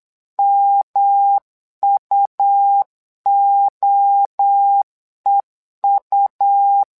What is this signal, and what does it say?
Morse code, heard as a keyed tone, about 8 wpm, 800 Hz. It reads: MUOEU
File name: LEZIONI / 3 MWRO / 3 MWRO 8wpm 2606